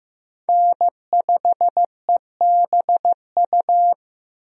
Morse code, heard as a keyed tone, about 15 wpm, 700 Hz. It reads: N5EBU
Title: USA Call Signs - 5 Characters - Repeated 3x - 15wpm